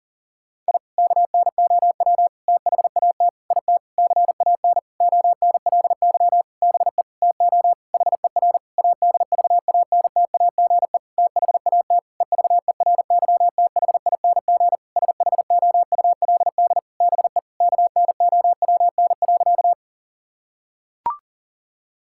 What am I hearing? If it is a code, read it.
Morse code, heard as a keyed tone, about 40 wpm, 700 Hz. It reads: I KNOW THAT IT CAN ONLY BE TO HER ADVANTAGE THAT EVERYTHING SHOULD BE KNOWN.
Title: Sentences from Top 1000 Words - Review Part 6 - 40wpm